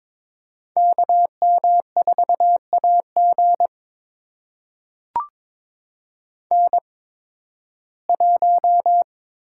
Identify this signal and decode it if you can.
Morse code, heard as a keyed tone, about 22 wpm, spaced as for 14 wpm, 700 Hz. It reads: KM4AG N1
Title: USA Call Signs - 8x Character Spacing - 22wpm